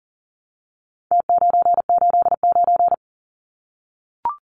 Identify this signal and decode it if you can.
Morse code, heard as a keyed tone, about 40 wpm, 700 Hz. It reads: T989
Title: Numbers - Contest Style - 4-Digit 40wpm